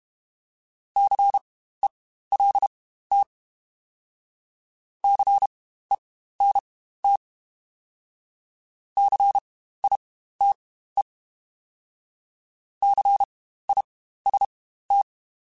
Morse code, heard as a keyed tone, about 32 wpm, spaced as for 15 wpm, 800 Hz. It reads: CELT CENT CITE CIST
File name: Spaced Alphas / Spaced Alphas mp3s / Cs sp @30